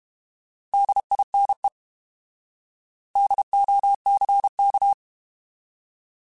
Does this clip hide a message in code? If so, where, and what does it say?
Yes: Morse code from 0.7 to 4.9 s, heard as a keyed tone, about 32 wpm, 800 Hz. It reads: DINE DOCK